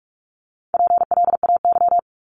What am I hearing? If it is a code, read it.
Morse code, heard as a keyed tone, about 46 wpm, 700 Hz. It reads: PLAY
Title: Top 200 Words 45wpm